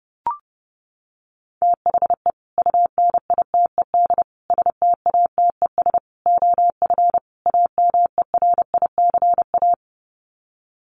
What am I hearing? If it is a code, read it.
Morse code, heard as a keyed tone, about 30 wpm, 700 Hz. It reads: THE UNITED STATES OF AMERICA
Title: Sets of 5 Words 30wpm